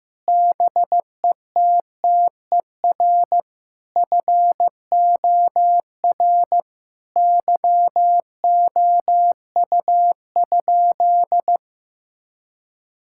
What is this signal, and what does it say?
Morse code, heard as a keyed tone, about 15 wpm, 700 Hz. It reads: BETTER FOR YOU?
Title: Sentences from Top 1000 Words - Review Part 5 - 15wpm